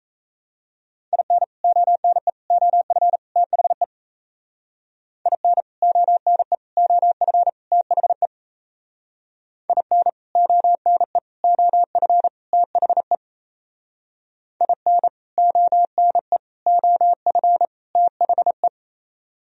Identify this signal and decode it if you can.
Morse code, heard as a keyed tone, about 35 wpm, 700 Hz. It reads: IN ONE OF THE IN ONE OF THE IN ONE OF THE IN ONE OF THE
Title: Speed-Racing - Sets of 4 Words 28wpm